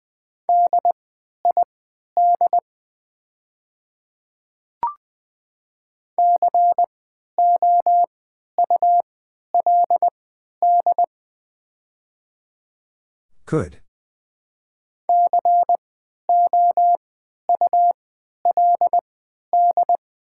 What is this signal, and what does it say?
Morse code, heard as a keyed tone, about 20 wpm, spaced as for 11 wpm, 700 Hz. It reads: DID COULD COULD
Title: 3x Character Spacing - Top 100 Words - 20wpm